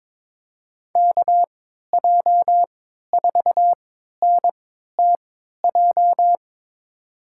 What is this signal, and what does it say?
Morse code, heard as a keyed tone, about 22 wpm, spaced as for 12 wpm, 700 Hz. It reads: KJ4NTJ
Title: USA Call Signs - 6 Characters - 3x Character Spacing - 22wpm